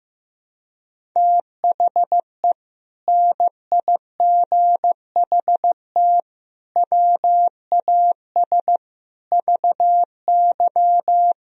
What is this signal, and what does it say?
Morse code, heard as a keyed tone, about 16 wpm, 700 Hz. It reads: THE NIGHT WAS VY
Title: Sentences from Top 500 Words and Abbreviations 15wpm